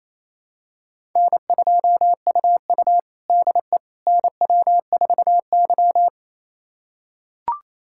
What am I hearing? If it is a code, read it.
Morse code, heard as a keyed tone, about 29 wpm, 690 Hz. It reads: N2UU DE NW4Y
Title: USA Callsign DE Callsign - 4 Characters - 28wpm